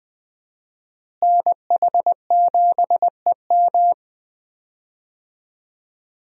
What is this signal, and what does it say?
Morse code, heard as a keyed tone, about 20 wpm, 700 Hz. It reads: NH7EM
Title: USA Call Signs - 5 Characters - Repeated 3x - 20wpm